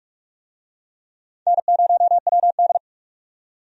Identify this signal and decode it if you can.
Morse code, heard as a keyed tone, about 44 wpm, 700 Hz. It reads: N0WD
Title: USA Call Signs - 4 Characters - 45wpm